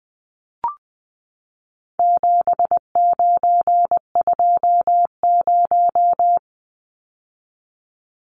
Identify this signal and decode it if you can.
Morse code, heard as a keyed tone, about 20 wpm, 700 Hz. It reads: 7920
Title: Numbers - 4-Digit 20wpm